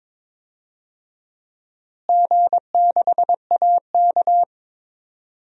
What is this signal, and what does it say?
Morse code, heard as a keyed tone, about 22 wpm, 700 Hz. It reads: G6AK
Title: DXCC-100 Call Signs - 4 Characters - Repeated 2x - 22wpm